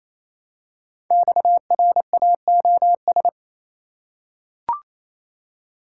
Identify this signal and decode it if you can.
Morse code, heard as a keyed tone, about 27 wpm, 700 Hz. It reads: XRAOS
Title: Code Groups - 5 Letters - 28wpm